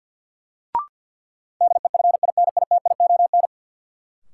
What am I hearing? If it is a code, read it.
Morse code, heard as a keyed tone, about 53 wpm, 700 Hz. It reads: DEFINITION
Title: Sentences from Top 1000 Words - Review Part 4 - New Word Drill - 50wpm